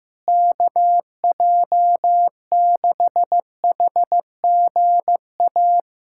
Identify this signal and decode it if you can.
Morse code, heard as a keyed tone, about 16 wpm, 700 Hz. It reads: KJ6HGA
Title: USA Callsign DE Callsign - 6 Characters - 15wpm